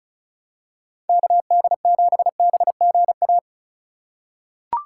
Morse code, heard as a keyed tone, about 36 wpm, 700 Hz. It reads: KD7BGA
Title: USA Call Signs - 6 Characters - 8x Character Spacing - 35wpm